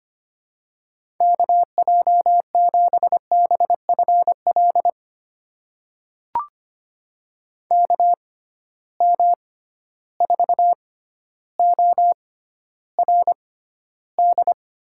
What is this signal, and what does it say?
Morse code, heard as a keyed tone, about 25 wpm, spaced as for 13 wpm, 700 Hz. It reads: KJ7BFL KM4ORD
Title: USA Call Signs - 6 Characters - 6x Character Spacing - 25wpm